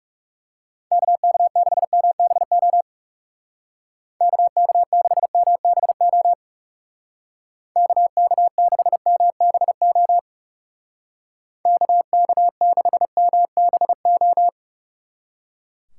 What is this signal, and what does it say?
Morse code, heard as a keyed tone, about 36 wpm, 700 Hz. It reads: KK6MBO KK6MBO KK6MBO KK6MBO
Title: Speed Racing - USA 2x3 Call Signs 30wpm